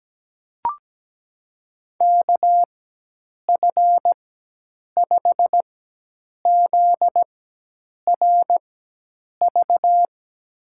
Morse code, heard as a keyed tone, about 17 wpm, spaced as for 8 wpm, 700 Hz. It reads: KF5ZRV
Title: USA Call Signs - 6 Characters - 4x Character Spacing - 17wpm